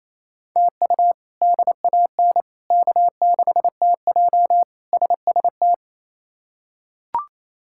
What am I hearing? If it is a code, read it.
Morse code, heard as a keyed tone, about 27 wpm, 700 Hz. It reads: TU DAN K6TJ SST
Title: K1USN SST QSO Practice - 28wpm